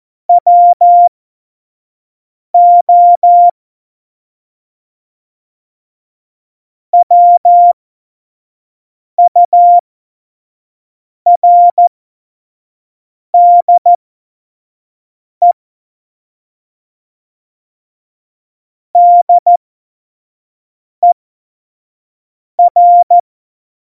Morse code, heard as a keyed tone, about 14 wpm, spaced as for 5 wpm, 700 Hz. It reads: WO WURDE DER
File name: Morsekurs / Witze / Witz 16. Juli 2019, 25 F70